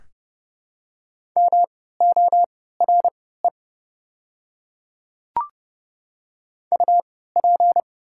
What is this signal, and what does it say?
Morse code, heard as a keyed tone, about 31 wpm, spaced as for 17 wpm, 700 Hz. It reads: MORE UP
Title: 3x Character Spacing - Top 100 Words - 30wpm